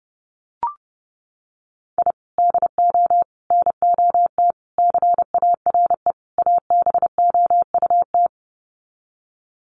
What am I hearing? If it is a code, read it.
Morse code, heard as a keyed tone, about 30 wpm, 700 Hz. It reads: I DO NOT CARE ABOUT
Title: Sets of 5 Words Encore2 30wpm